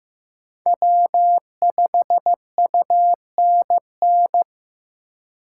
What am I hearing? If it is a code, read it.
Morse code, heard as a keyed tone, about 15 wpm, 700 Hz. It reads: W5UNN
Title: USA Call Signs - 5 Characters - 4x Character Spacing - 15wpm